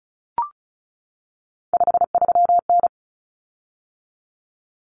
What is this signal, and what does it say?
Morse code, heard as a keyed tone, about 34 wpm, 700 Hz. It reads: 53N
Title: Common Cut Numbers - 3-Digit 35wpm